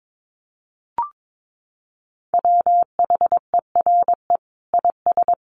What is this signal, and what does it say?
Morse code, heard as a keyed tone, about 22 wpm, 700 Hz. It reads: WHERE IS
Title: Sentences from Top 1500 Words 22wpm